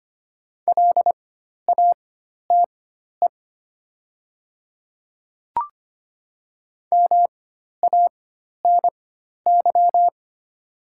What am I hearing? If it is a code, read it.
Morse code, heard as a keyed tone, about 25 wpm, spaced as for 12 wpm, 700 Hz. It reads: LATE MANY
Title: 4x Character Spacing - Top 300 Words - 25wpm